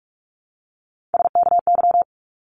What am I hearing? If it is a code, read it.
Morse code, heard as a keyed tone, about 46 wpm, 700 Hz. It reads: SKY
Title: Multi-letter Phonemes 45wpm